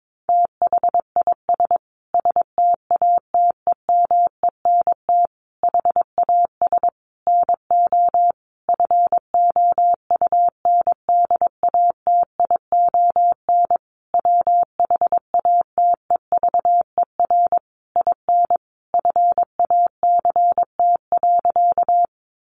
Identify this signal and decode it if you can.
Morse code, heard as a keyed tone, about 22 wpm, 700 Hz. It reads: THIS STATEMENT HAS NO FOUNDATION WHATEVER IN FACT.
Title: Sentences from Top 1200 Words 22wpm